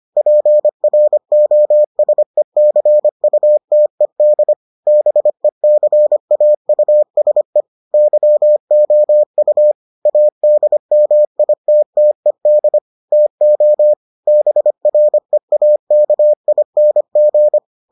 Morse code, heard as a keyed tone, about 25 wpm, 600 Hz. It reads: PROSECUTED BECAUSE YOU ADMITTED TO BREAKING